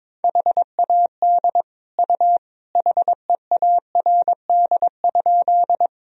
Morse code, heard as a keyed tone, about 22 wpm, 700 Hz. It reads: HAD U HEARD?
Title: Sentences from Top 400 Words and Abbreviations 22wpm